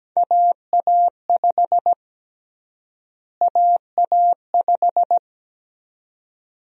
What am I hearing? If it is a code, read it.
Morse code, heard as a keyed tone, about 17 wpm, 700 Hz. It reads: AA5 AA5
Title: USA Call Sign Prefixes - Repeated 2x - 17wpm